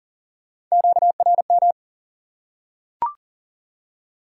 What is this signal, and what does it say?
Morse code, heard as a keyed tone, about 42 wpm, 700 Hz. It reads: QRM
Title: Top 180 QSO Elements 40wpm